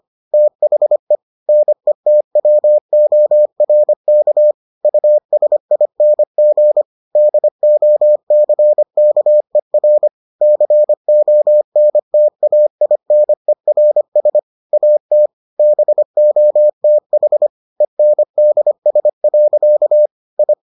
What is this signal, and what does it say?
Morse code, heard as a keyed tone, about 25 wpm, 600 Hz. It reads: THE NETWORK USING DOCKER CONTAINERS AT BOTH ENDS. I